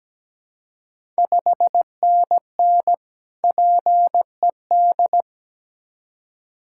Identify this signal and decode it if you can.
Morse code, heard as a keyed tone, about 17 wpm, 700 Hz. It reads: 5NN PED
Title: Ontario QSO Party Exchange 17wpm